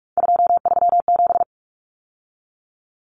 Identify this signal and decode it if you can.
Morse code, heard as a keyed tone, about 44 wpm, 700 Hz. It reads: 237
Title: Numbers - 3-Digit 45wpm